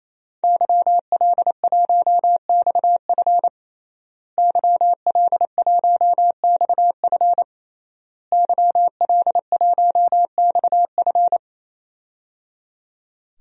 Morse code, heard as a keyed tone, about 28 wpm, 700 Hz. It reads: YL1XF YL1XF YL1XF
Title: DXCC-100 Call Signs - 5 Characters - Repeated 3x - 28wpm